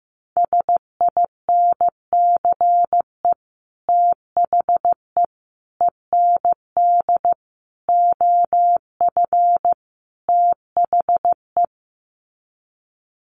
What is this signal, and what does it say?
Morse code, heard as a keyed tone, about 15 wpm, 700 Hz. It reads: SINCE THE END OF THE
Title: Sets of 5 Words Encore 15wpm